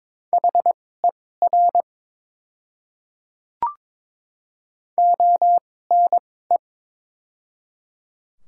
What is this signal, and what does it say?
Morse code, heard as a keyed tone, about 22 wpm, spaced as for 16 wpm, 700 Hz. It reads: HER ONE
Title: 2x Character Spacing - Top 200 Words - 22wpm